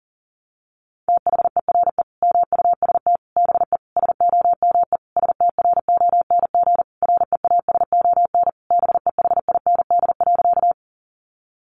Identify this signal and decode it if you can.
Morse code, heard as a keyed tone, about 40 wpm, 700 Hz. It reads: THERE MUST BE SOME STRONG REASON BEHIND.
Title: Sentences from Top 700 Words 40wpm